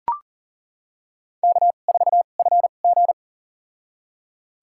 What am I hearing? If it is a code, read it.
Morse code, heard as a keyed tone, about 40 wpm, spaced as for 29 wpm, 700 Hz. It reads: K4FG
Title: USA Call Signs - 4 Characters - 2x Character Spacing - 40wpm